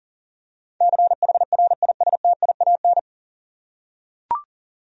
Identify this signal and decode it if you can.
Morse code, heard as a keyed tone, about 40 wpm, 700 Hz. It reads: CHRISTIAN
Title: USA Common Names - 40wpm